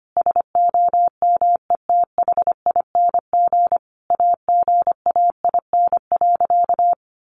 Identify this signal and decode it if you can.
Morse code, heard as a keyed tone, about 25 wpm, 700 Hz. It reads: SOMETHING AGAIN.
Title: Sentences from Top 1100 Words 25wpm